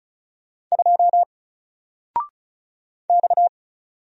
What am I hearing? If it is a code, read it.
Morse code, heard as a keyed tone, about 36 wpm, 700 Hz. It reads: 2 X
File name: Single-Letter-Number-35wpm